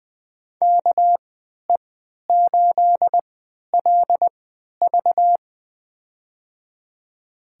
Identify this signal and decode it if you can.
Morse code, heard as a keyed tone, about 21 wpm, spaced as for 12 wpm, 700 Hz. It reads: KE8LV
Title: USA Call Signs - 5 Characters - 3x Character Spacing - 20wpm